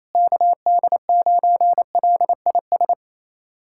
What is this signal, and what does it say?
Morse code, heard as a keyed tone, about 28 wpm, 700 Hz. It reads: KD9LIS
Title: USA Callsign DE Callsign 28wpm